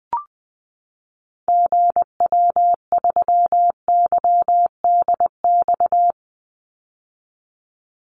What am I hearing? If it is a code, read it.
Morse code, heard as a keyed tone, about 20 wpm, 700 Hz. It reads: GW3YDX
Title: Call Signs - 20wpm